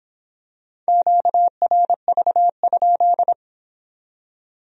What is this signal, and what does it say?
Morse code, heard as a keyed tone, about 25 wpm, 700 Hz. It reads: QRV?